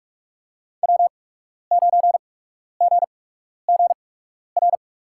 Morse code, heard as a keyed tone, about 41 wpm, spaced as for 12 wpm, 700 Hz. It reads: W9GGR
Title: USA Call Signs - 5 Characters - 8x Character Spacing - 45wpm